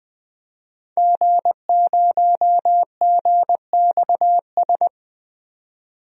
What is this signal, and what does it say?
Morse code, heard as a keyed tone, about 20 wpm, 700 Hz. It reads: G0GXS
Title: DXCC-100 Call Signs - 5 Characters - 20wpm